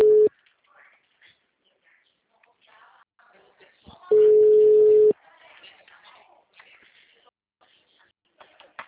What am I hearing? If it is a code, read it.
Ringback tone (Europe, 425 Hz), with speech.
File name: out- -2017- - - .99829